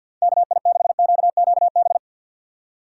Morse code, heard as a keyed tone, about 52 wpm, 700 Hz. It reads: KI6QQB
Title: USA Call Signs - 6 Characters - 8x Character Spacing - 50wpm